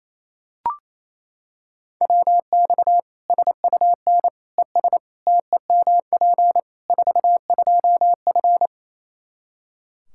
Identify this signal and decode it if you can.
Morse code, heard as a keyed tone, about 28 wpm, 700 Hz. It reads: WX SUN ES TEMP 42F
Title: K1USN SST QSO Practice 28wpm